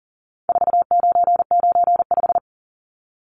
Morse code, heard as a keyed tone, about 40 wpm, 700 Hz. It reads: 4995
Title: Numbers - 4-Digit 40wpm